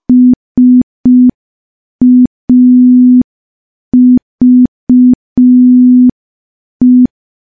Morse code, heard as a keyed tone, about 5 wpm, 250 Hz. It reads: SAVE
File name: morse / morse-training-05WPM